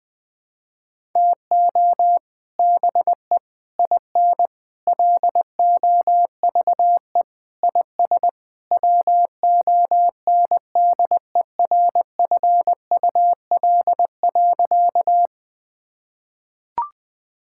Morse code, heard as a keyed tone, about 20 wpm, 700 Hz. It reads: TO BE IN LOVE IS WONDERFUL.
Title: Sentences with 3-Word Phrases - 20wpm